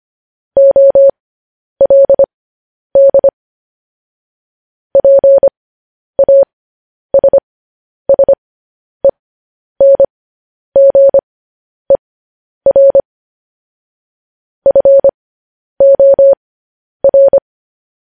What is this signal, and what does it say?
Morse code, heard as a keyed tone, about 25 wpm, spaced as for 10 wpm, 550 Hz. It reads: OLD PASSENGER FOR